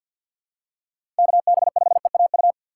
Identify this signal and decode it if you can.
Morse code, heard as a keyed tone, about 53 wpm, 700 Hz. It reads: KB5EAU
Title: USA Call Signs - 6 Characters - 3x Character Spacing - 50wpm